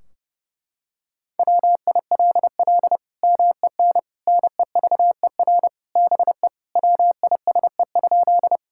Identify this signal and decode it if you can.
Morse code, heard as a keyed tone, about 30 wpm, 700 Hz. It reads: WILL MEN NEVER BE WISE?
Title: Sentences from Top 1600 Words 30wpm